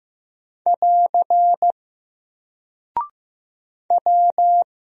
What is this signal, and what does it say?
Morse code, heard as a keyed tone, about 15 wpm, 700 Hz. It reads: <AR> W
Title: Single Character (Letter:Number:Punctuation:Prosign) Alt 15wpm